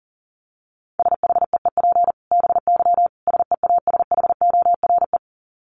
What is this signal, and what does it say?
Morse code, heard as a keyed tone, about 40 wpm, 700 Hz. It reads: SHEEP BY SEASHORE
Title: Dit Challenge 40wpm